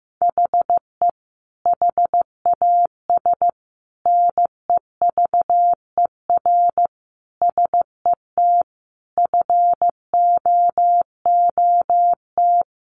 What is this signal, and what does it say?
Morse code, heard as a keyed tone, about 15 wpm, 700 Hz. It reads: HE HAS NEVER SET FOOT
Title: Sentences from Top 1500 Words 15wpm